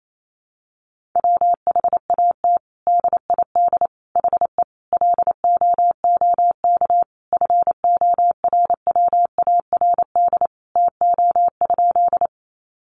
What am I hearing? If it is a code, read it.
Morse code, heard as a keyed tone, about 28 wpm, 700 Hz. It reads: WHAT DID HE LOOK FORWARD TO?
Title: Sentences from Top 1300 Words 28wpm